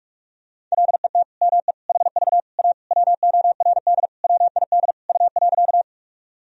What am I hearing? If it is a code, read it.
Morse code, heard as a keyed tone, about 45 wpm, 700 Hz. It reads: LET ME HV A WORD WID U.
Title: Sentences from Top 300 Words and Abbreviations 45wpm